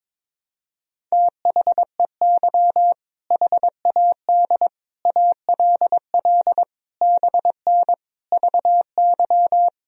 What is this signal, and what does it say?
Morse code, heard as a keyed tone, about 22 wpm, 700 Hz. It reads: THEY HAD ALL BN VY
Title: Sentences from Top 200 Words and Abbreviations 22wpm